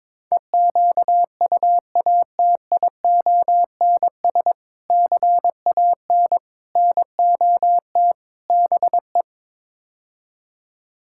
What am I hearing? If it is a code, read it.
Morse code, heard as a keyed tone, about 22 wpm, 700 Hz. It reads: EQUATIONS CAN NOT BE
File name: practice / 22wpm / Build A Phrase - 5-Word Phrases - 22wpm